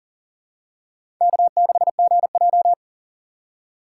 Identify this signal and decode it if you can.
Morse code, heard as a keyed tone, about 38 wpm, 700 Hz. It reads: K6GJ
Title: USA Call Signs - 2x Character Spacing - 40wpm